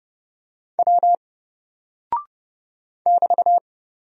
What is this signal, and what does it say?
Morse code, heard as a keyed tone, about 29 wpm, 700 Hz. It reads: W <BT>
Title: Single Character (Letter:Number:Punctuation:Prosign) 30wpm